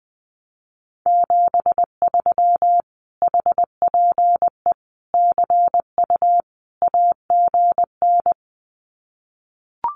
Morse code, heard as a keyed tone, about 19 wpm, 700 Hz. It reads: 73 HPE CU AGN
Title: Common QSO Phrases Encore 20wpm